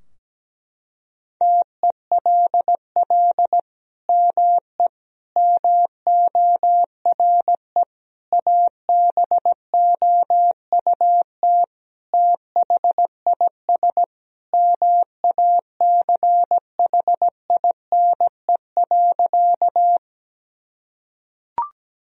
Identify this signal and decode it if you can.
Morse code, heard as a keyed tone, about 17 wpm, 700 Hz. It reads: TELL ME MORE ABOUT THIS MACHINE.
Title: Sentences from Top 1000 Words - Review Part 2 - 17wpm